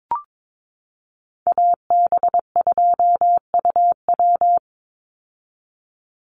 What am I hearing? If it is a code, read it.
Morse code, heard as a keyed tone, about 22 wpm, 700 Hz. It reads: AB2UW
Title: USA Call Signs 22wpm